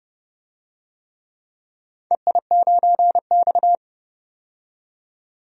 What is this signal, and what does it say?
Morse code, heard as a keyed tone, about 30 wpm, 700 Hz. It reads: EI9X
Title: DXCC-100 Call Signs - 4 Characters - 30wpm